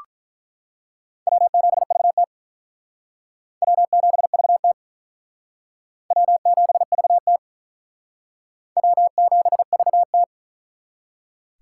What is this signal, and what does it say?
Morse code, heard as a keyed tone, about 44 wpm, 700 Hz. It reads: W7VT W7VT W7VT W7VT